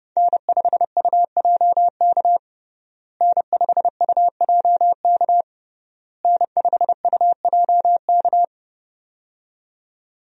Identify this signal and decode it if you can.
Morse code, heard as a keyed tone, about 30 wpm, 700 Hz. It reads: N5UJK N5UJK N5UJK
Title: USA Call Signs - 5 Characters - Repeated 3x - 30wpm